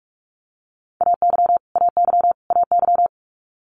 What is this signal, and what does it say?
Morse code, heard as a keyed tone, about 46 wpm, 700 Hz. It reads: AY AY AY
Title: Multi-letter Phonemes Repeated 3x 45wpm